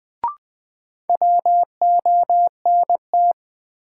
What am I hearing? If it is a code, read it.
Morse code, heard as a keyed tone, about 20 wpm, 700 Hz. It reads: WONT
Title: Sentences from Top 1000 Words - Review Part 6 - New Word Drill - 20wpm